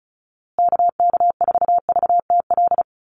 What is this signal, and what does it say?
Morse code, heard as a keyed tone, about 34 wpm, 700 Hz. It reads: KK4VTL
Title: USA Call Signs Repeated 2x 35wpm